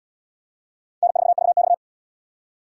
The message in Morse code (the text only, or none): N4VX